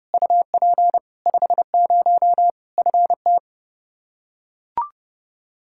Morse code, heard as a keyed tone, about 30 wpm, 700 Hz. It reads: UP 50 FT
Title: Practice QSO - Exchange 2 - Info - 30wpm